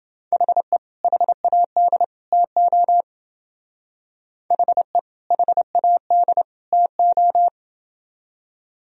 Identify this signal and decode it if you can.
Morse code, heard as a keyed tone, about 28 wpm, 700 Hz. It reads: HE HAD TO HE HAD TO